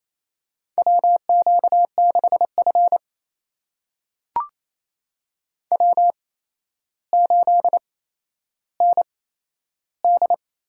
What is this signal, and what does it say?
Morse code, heard as a keyed tone, about 28 wpm, spaced as for 12 wpm, 700 Hz. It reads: WQ6F W8ND
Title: USA Call Signs - 8x Character Spacing - 28wpm